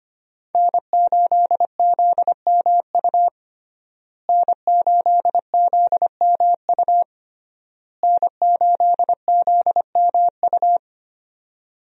Morse code, heard as a keyed tone, about 25 wpm, 700 Hz. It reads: N8ZMU N8ZMU N8ZMU
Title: USA Call Signs - 5 Characters - Repeated 3x - 25wpm